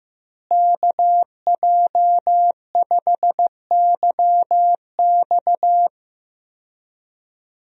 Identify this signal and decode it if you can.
Morse code, heard as a keyed tone, about 16 wpm, 700 Hz. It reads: KJ5YX
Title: USA Call Signs - 5 Characters - Repeated 2x - 15wpm